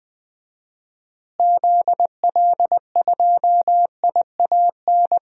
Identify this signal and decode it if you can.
Morse code, heard as a keyed tone, about 20 wpm, 700 Hz. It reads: ZL2IAN
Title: DXCC-100 Call Signs - 6 Characters - Repeated 3x - 20wpm